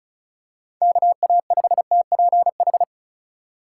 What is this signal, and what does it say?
Morse code, heard as a keyed tone, about 34 wpm, 700 Hz. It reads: KA5TPH
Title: USA Call Signs - 2x Character Spacing - 35wpm